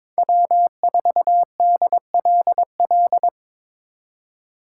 Morse code, heard as a keyed tone, about 25 wpm, 700 Hz. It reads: W4DLL
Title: Speed Racing - USA Call Signs 22wpm